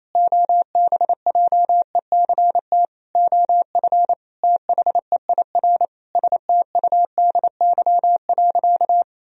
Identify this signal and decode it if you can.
Morse code, heard as a keyed tone, about 27 wpm, 700 Hz. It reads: OBJECT OF THEIR STUDY.
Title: Sentences from Top 1000 Words - Review Part 3 - 28wpm